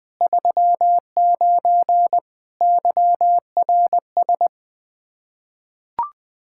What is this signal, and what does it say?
Morse code, heard as a keyed tone, about 20 wpm, 700 Hz. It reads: 39 YRS
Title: Practice QSO - Exchange 3 - More Info - 20wpm